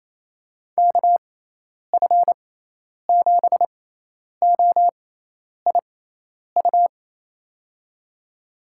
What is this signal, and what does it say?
Morse code, heard as a keyed tone, about 28 wpm, spaced as for 10 wpm, 700 Hz. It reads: KF7OIU
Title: USA Call Signs - 6 Characters - 6x Character Spacing - 28wpm